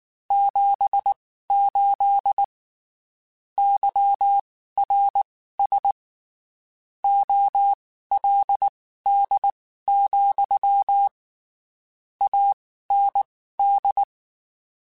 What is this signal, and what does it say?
Morse code, heard as a keyed tone, about 19 wpm, spaced as for 14 wpm, 800 Hz. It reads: 78 YRS OLD, AND
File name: (spaced) / QSOs Spaced-mp3s / kc7haa @18 sp